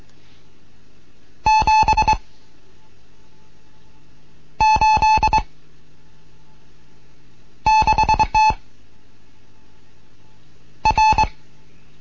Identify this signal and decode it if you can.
Morse code, heard as a keyed tone, about 22 wpm, 870 Hz. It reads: 7 8 6 T R